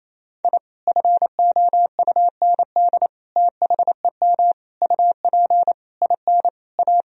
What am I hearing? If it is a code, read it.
Morse code, heard as a keyed tone, about 28 wpm, 700 Hz. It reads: I FOUND THEM UP IN A
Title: Sentences from Top 1000 Words - Review Part 1 - 28wpm